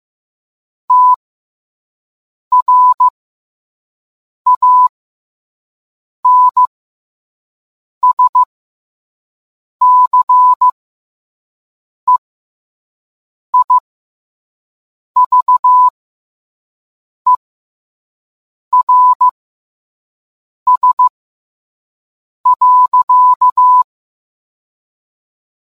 Morse code, heard as a keyed tone, about 15 wpm, spaced as for 5 wpm, 1000 Hz. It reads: TRANSCEIVERS.